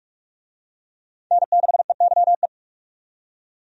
Morse code, heard as a keyed tone, about 46 wpm, 700 Hz. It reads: N6EYE
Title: USA Call Signs - 5 Characters - 45wpm